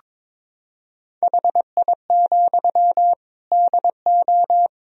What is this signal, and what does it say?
Morse code, heard as a keyed tone, about 23 wpm, 700 Hz. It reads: HI, DO
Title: Sentences from Top 1000 Words - Review Part 3 - 22wpm